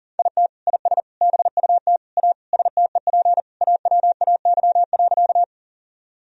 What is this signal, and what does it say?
Morse code, heard as a keyed tone, about 40 wpm, 700 Hz. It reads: IT IS BUT A STEP AWAY.
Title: Sentences from Top 1000 Words - Review Part 2 - 40wpm